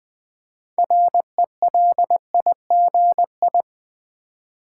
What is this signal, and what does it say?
Morse code, heard as a keyed tone, about 20 wpm, 700 Hz. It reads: RELIGI
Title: Build A Word - 3 Letters At A Time - 9-Letter Words - 20wpm